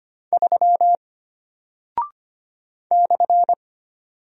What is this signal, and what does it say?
Morse code, heard as a keyed tone, about 25 wpm, 700 Hz. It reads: 3 /
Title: Single Character (Letter:Number:Punctuation:Prosign) Alt 25wpm